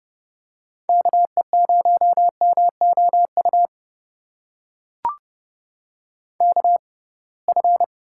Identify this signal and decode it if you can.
Morse code, heard as a keyed tone, about 30 wpm, spaced as for 23 wpm, 700 Hz. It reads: KE0MOU KF
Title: USA Call Signs - 6 Characters - 6x Character Spacing - 30wpm